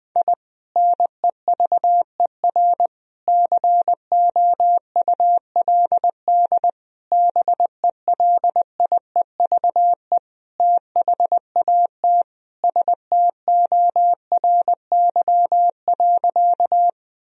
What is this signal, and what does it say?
Morse code, heard as a keyed tone, about 20 wpm, 700 Hz. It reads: I NEVER COULD BELIEVE THAT STORY.
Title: Sentences from Top 1000 Words - Review Part 1 - 20wpm